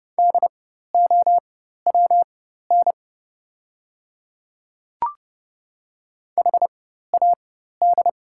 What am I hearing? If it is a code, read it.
Morse code, heard as a keyed tone, about 30 wpm, spaced as for 14 wpm, 690 Hz. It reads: DOWN HAD
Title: 4x Character Spacing - Top 100 Words - 30wpm